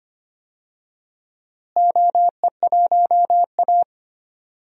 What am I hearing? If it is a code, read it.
Morse code, heard as a keyed tone, about 25 wpm, 700 Hz. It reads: OE1A
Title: DXCC-100 Call Signs - 4 Characters - Repeated 2x - 25wpm